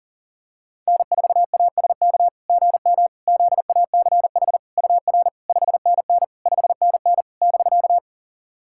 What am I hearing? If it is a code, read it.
Morse code, heard as a keyed tone, about 37 wpm, 690 Hz. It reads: N4ASK GM ZACH UR 5NN 5NN <BK>
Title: SOTA QSO Practice 40wpm